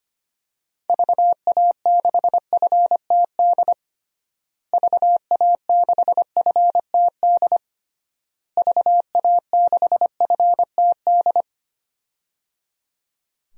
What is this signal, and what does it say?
Morse code, heard as a keyed tone, about 25 wpm, 700 Hz. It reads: VA6FTD VA6FTD VA6FTD
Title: DXCC-100 Call Signs - 6 Characters - Repeated 3x - 25wpm